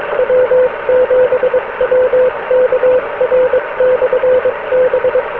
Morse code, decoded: W7WKR/B